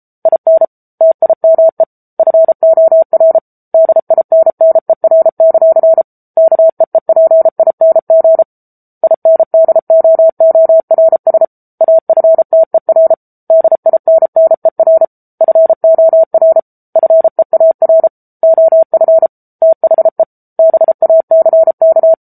Morse code, heard as a keyed tone, about 34 wpm, 650 Hz. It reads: IN TIME FOR DINNER; KEEPING INDOORS AFTER DINNER FOR FEAR OF THE BACK
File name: Book Two / 07-Chapter 7 - 33 WPM